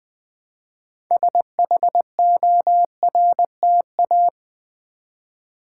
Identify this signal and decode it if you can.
Morse code, heard as a keyed tone, about 20 wpm, 700 Hz. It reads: SHORTA
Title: Build A Word - 2 Letters At A Time - 8-Letter Words - 20wpm